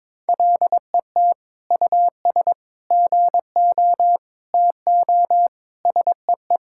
Morse code, heard as a keyed tone, about 22 wpm, 700 Hz. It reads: LET US GO TO SEE
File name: practice / 22wpm / Sentences with 8-Word Phrases - 22wpm